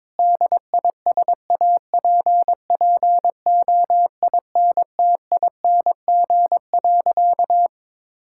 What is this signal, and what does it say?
Morse code, heard as a keyed tone, about 22 wpm, 700 Hz. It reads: DISAPPOINTING.
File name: practice / 22wpm / Sentences with 4-Word Phrases - 22wpm